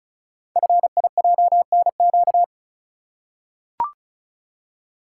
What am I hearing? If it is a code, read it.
Morse code, heard as a keyed tone, about 36 wpm, 690 Hz. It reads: FIJNQ
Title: Code Groups - 5 Letters - 35wpm